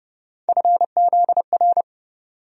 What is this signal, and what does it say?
Morse code, heard as a keyed tone, about 30 wpm, 700 Hz. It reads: FZR
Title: USA Call Sign Postfixes - 3 Letters 30wpm